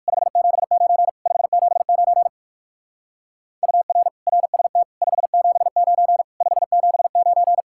Morse code, heard as a keyed tone, about 64 wpm, spaced as for 56 wpm, 700 Hz. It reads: 579 579 UR RST 579 579
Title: Speed-Racing Common QSO Phrases - 45wpm